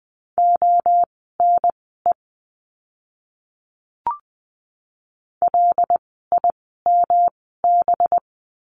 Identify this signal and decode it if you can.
Morse code, heard as a keyed tone, about 20 wpm, spaced as for 15 wpm, 700 Hz. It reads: ONE LIMB
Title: 185 Lesson 30f - Letter b - Words 20wpm